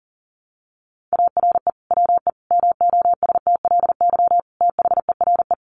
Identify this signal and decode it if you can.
Morse code, heard as a keyed tone, about 40 wpm, 700 Hz. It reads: ARE WE MOSTLY THERE
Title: Sentences from Top 1300 Words 40wpm